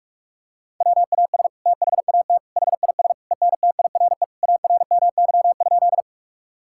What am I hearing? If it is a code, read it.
Morse code, heard as a keyed tone, about 47 wpm, 700 Hz. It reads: WAS THAT HIS ENTIRE ARMY?
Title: Sentences from Top 1000 Words - Review Part 5 - 45wpm